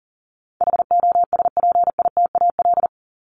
Morse code, heard as a keyed tone, about 40 wpm, 700 Hz. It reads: HOSPITAL
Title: Top 1751-2000 Words 40wpm